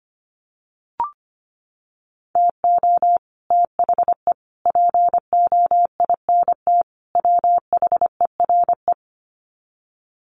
Morse code, heard as a keyed tone, about 25 wpm, 700 Hz. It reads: TO THE POINT WHERE
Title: Sets of 4 Words 25wpm